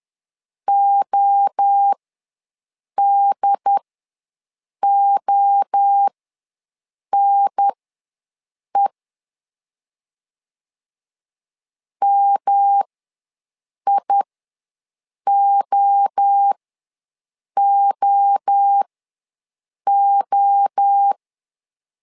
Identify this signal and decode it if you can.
Morse code, heard as a keyed tone, about 10 wpm, spaced as for 6 wpm, 800 Hz. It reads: ODONE MIOOO
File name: LEZIONI / 3 MWRO / 3 MWRO ws 10wpm 2310